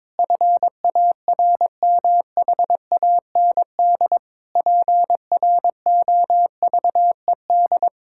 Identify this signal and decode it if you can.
Morse code, heard as a keyed tone, about 22 wpm, 700 Hz. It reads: FARMHAND PROVED
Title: Pangram Challenge v2 - 22wpm